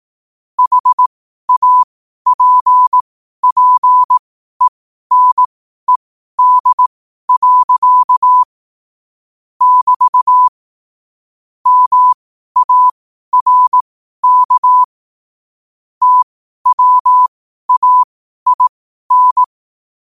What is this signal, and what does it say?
Morse code, heard as a keyed tone, about 18 wpm, spaced as for 12 wpm, 1000 Hz. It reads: HAPPENED. <BT> MARK TWAIN
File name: qotdfiles / 12wpm